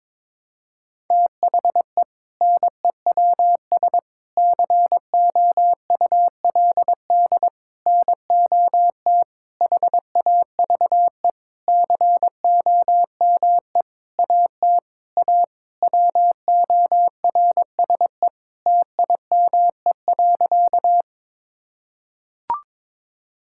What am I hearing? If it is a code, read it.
Morse code, heard as a keyed tone, about 22 wpm, 700 Hz. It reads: THE NEWS COULD NOT HAVE COME AT A WORSE TIME.
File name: practice / 22wpm / Sentences with 8-Word Phrases - 22wpm